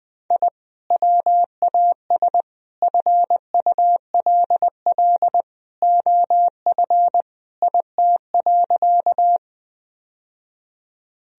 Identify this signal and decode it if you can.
Morse code, heard as a keyed tone, about 20 wpm, 700 Hz. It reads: I WAS FULL OF IT.
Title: Sentences from Top 1000 Words - Review Part 3 - 20wpm